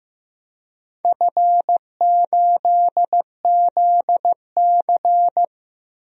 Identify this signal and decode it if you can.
Morse code, heard as a keyed tone, about 15 wpm, 700 Hz. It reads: F8ZC
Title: DXCC-100 Call Signs - 4 Characters - Repeated 3x - 15wpm